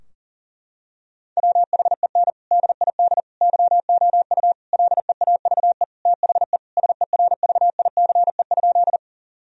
Morse code, heard as a keyed tone, about 40 wpm, 700 Hz. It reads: WHEN DID YOU LEAVE THE SERVICE?
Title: Sentences from Top 1000 Words - Review Part 4 - 40wpm